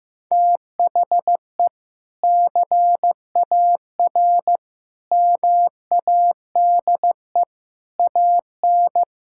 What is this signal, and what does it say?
Morse code, heard as a keyed tone, about 16 wpm, 700 Hz. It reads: THE CAR MADE AN
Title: Sentences from Top 1000 Words - Review Part 6 - 15wpm